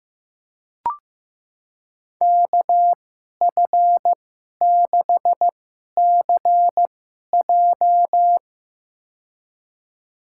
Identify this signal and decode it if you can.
Morse code, heard as a keyed tone, about 15 wpm, spaced as for 11 wpm, 700 Hz. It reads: KF6CJ
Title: USA Call Signs - 2x Character Spacing - 15wpm